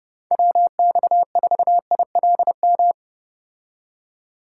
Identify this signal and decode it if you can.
Morse code, heard as a keyed tone, about 29 wpm, 700 Hz. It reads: WX4ILM